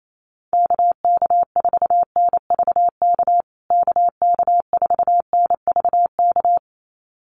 Morse code, heard as a keyed tone, about 28 wpm, 700 Hz. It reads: KK4NVK KK4NVK
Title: USA Call Signs Repeated 2x 28wpm